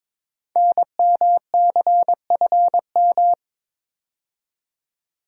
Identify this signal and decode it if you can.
Morse code, heard as a keyed tone, about 22 wpm, 700 Hz. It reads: NMCFM
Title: Code Groups - 5 Letters - 22wpm